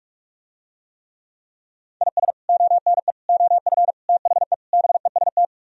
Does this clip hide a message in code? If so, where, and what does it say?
Yes: Morse code from 2.0 to 5.4 s, heard as a keyed tone, about 45 wpm, 700 Hz. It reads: IS ONE OF THE BEST